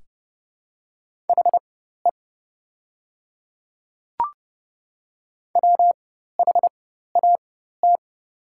Morse code, heard as a keyed tone, about 30 wpm, spaced as for 14 wpm, 700 Hz. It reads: HE WHAT
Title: 4x Character Spacing - Top 200 Words - 30wpm